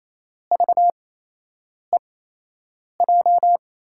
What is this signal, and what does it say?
Morse code, heard as a keyed tone, about 28 wpm, spaced as for 8 wpm, 700 Hz. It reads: VEJ
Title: USA Call Signs - 6 Characters - 8x Character Spacing - 28wpm